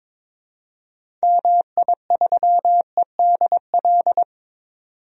Morse code, heard as a keyed tone, about 21 wpm, 700 Hz. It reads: MI3EDL
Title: DXCC-100 Call Signs - 6 Characters - 22wpm